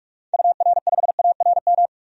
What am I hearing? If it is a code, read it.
Morse code, heard as a keyed tone, about 41 wpm, 700 Hz. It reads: UR5ARM
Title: DXCC-100 Call Signs - 6 Characters - 45wpm